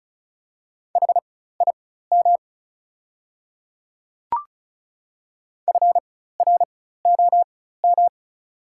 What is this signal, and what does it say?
Morse code, heard as a keyed tone, about 34 wpm, spaced as for 16 wpm, 700 Hz. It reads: HIM FROM
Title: 4x Character Spacing - Top 300 Words - 35wpm